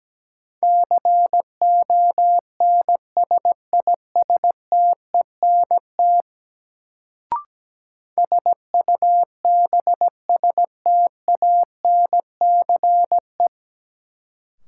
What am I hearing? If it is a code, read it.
Morse code, heard as a keyed tone, about 17 wpm, 700 Hz. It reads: CONSISTENT SUBSTANCE
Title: Sentences from Top 1500 Words - New Word Drill Alt - 17wpm